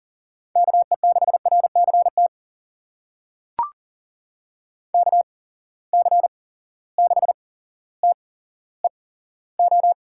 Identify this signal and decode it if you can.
Morse code, heard as a keyed tone, about 40 wpm, spaced as for 17 wpm, 700 Hz. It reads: KE6RCT KC6TEO
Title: USA Call Signs - 6 Characters - 8x Character Spacing - 40wpm